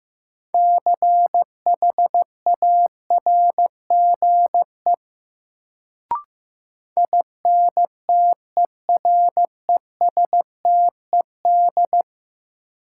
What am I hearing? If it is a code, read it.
Morse code, heard as a keyed tone, about 15 wpm, 700 Hz. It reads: CHARGE INTERESTED
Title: Sentences from Top 900 Words - New Word Drill - 15wpm